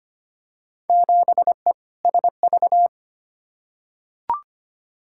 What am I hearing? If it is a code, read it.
Morse code, heard as a keyed tone, about 25 wpm, 700 Hz. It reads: 7E SV
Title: Field Day Exchanges Encore - 25wpm